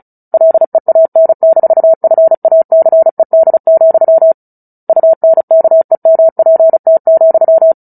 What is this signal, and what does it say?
Morse code, heard as a keyed tone, about 34 wpm, 650 Hz. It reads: LEAN-FACED, UNKEMPT,
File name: Book Two / 09-Chapter 9 - 35 WPM